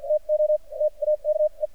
Morse code, heard as a keyed tone, about 48 wpm, spaced as for 37 wpm, 610 Hz. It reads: TO TAKE
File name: archived / CRSnet